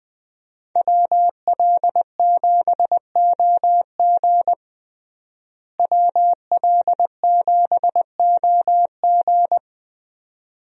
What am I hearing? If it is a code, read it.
Morse code, heard as a keyed tone, about 21 wpm, 700 Hz. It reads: WL7OG WL7OG
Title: USA Call Signs - 5 Characters - Repeated 2x - 20wpm